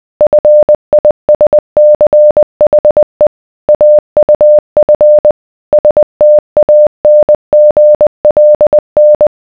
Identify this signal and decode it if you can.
Morse code, heard as a keyed tone, about 20 wpm, 600 Hz. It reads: FISCHE AUF STANGLN